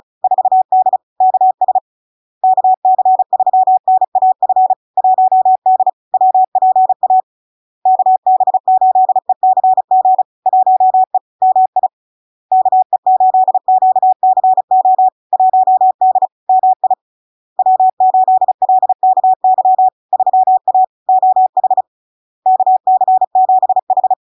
Morse code, heard as a keyed tone, about 36 wpm, 750 Hz. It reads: 4D KS KC3NAF 1D WPA KB8ECG 1E MI KE8QCO 1D MI W8LKY 3A OH KC7H
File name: LARK-FD-2022-Log-in-CW-35wpm-750-Hz